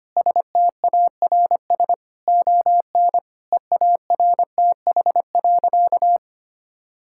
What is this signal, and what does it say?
Morse code, heard as a keyed tone, about 25 wpm, 700 Hz. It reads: STARS ON EARTH.
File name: practice / 25wpm / Difficult Sentences with Hints - 25wpm